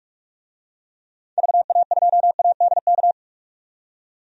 Morse code, heard as a keyed tone, about 47 wpm, 700 Hz. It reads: VA2ADK
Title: DXCC-100 Call Signs - 6 Characters - 45wpm